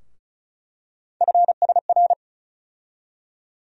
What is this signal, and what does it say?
Morse code, heard as a keyed tone, about 36 wpm, 700 Hz. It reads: FSR
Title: USA Call Sign Postfixes - 3 Letters 35wpm